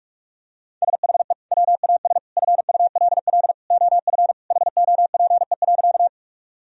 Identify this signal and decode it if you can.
Morse code, heard as a keyed tone, about 42 wpm, 700 Hz. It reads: SHE WAS FULL OF HOPE.
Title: Sentences from Top 1000 Words - Review Part 3 - 45wpm